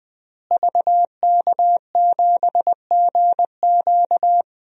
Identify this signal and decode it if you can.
Morse code, heard as a keyed tone, about 20 wpm, 700 Hz. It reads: VK7GQ
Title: DXCC-100 Call Signs - 5 Characters - Repeated 2x - 20wpm